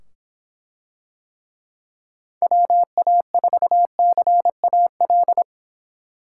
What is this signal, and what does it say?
Morse code, heard as a keyed tone, about 25 wpm, 700 Hz. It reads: WA4CAL